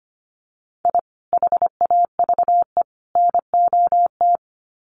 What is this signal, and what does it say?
Morse code, heard as a keyed tone, about 25 wpm, 700 Hz. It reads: I HAVE NOT
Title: Sets of 3 Words 25wpm